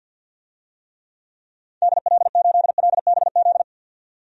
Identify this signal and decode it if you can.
Morse code, heard as a keyed tone, about 51 wpm, 700 Hz. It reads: DL8LBZ
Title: DXCC-100 Call Signs - 6 Characters - 50wpm